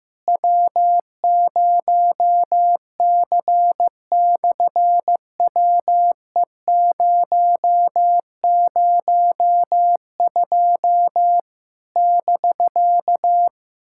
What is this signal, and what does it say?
Morse code, heard as a keyed tone, about 15 wpm, 700 Hz. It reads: W0C/WE002 <BK>
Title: SOTA QSO Practice 15wpm